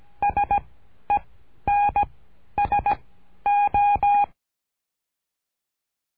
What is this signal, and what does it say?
Morse code, heard as a keyed tone, about 17 wpm, 820 Hz. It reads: S E N S O
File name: cursos / CURSO DE TELEGRAFIA / Telegrafia - Trilha 13